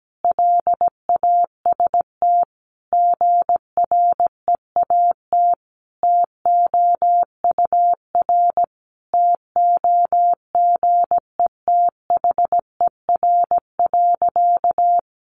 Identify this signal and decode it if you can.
Morse code, heard as a keyed tone, about 17 wpm, 700 Hz. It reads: LAST GREAT TOUR TOGETHER.
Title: Sentences from Top 1400 Words 17wpm